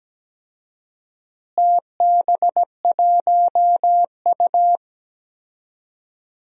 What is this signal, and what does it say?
Morse code, heard as a keyed tone, about 17 wpm, 700 Hz. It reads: TB1U
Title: DXCC-100 Call Signs - 4 Characters - Repeated 2x - 17wpm